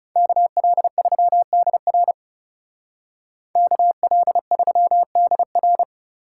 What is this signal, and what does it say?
Morse code, heard as a keyed tone, about 37 wpm, 700 Hz. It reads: KL3DR KL3DR
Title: Speed Racing - USA 2x2 Call Signs 30wpm